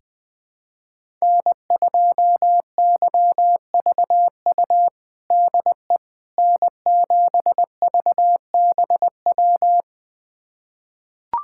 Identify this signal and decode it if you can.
Morse code, heard as a keyed tone, about 20 wpm, 700 Hz. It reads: N2YVU DE N7VBW
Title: USA Callsign DE Callsign 20wpm